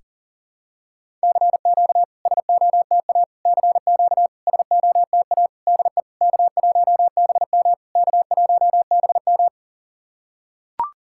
Morse code, heard as a keyed tone, about 40 wpm, 700 Hz. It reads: CQ SOTA CQ SOTA DE K1BM K1BM
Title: SOTA QSO Practice 40wpm